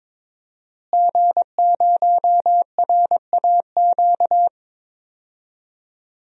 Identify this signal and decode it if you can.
Morse code, heard as a keyed tone, about 21 wpm, 700 Hz. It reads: G0RAQ
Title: DXCC-100 Call Signs - 5 Characters - Repeated 2x - 22wpm